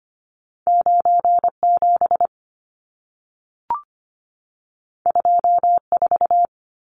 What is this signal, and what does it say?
Morse code, heard as a keyed tone, about 25 wpm, 700 Hz. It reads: 97 24
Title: Numbers - 2-Digit 25wpm